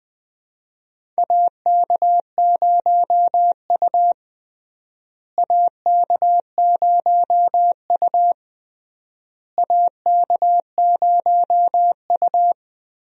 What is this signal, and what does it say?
Morse code, heard as a keyed tone, about 21 wpm, 700 Hz. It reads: AK0U AK0U AK0U
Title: USA Call Signs - 4 Characters - Repeated 3x - 20wpm